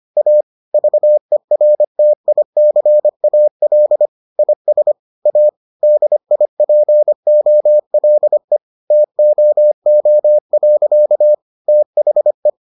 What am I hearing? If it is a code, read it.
Morse code, heard as a keyed tone, about 25 wpm, 600 Hz. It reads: A VERTICAL IS A DIPOLE TOO. THE